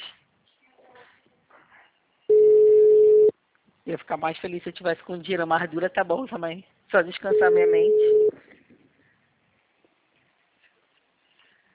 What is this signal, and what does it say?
Ringback tone (Europe, 425 Hz), with speech.
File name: out- -2032- - - .343